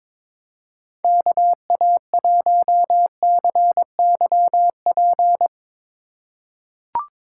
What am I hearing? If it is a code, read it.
Morse code, heard as a keyed tone, about 22 wpm, 700 Hz. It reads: KA1CYP
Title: USA Call Signs - 6 Characters - 2x Character Spacing - 22wpm